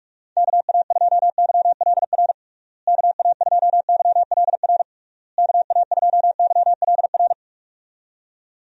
Morse code, heard as a keyed tone, about 46 wpm, 700 Hz. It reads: KA2YLR KA2YLR KA2YLR
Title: USA Call Signs - 6 Characters - Repeated 3x - 45wpm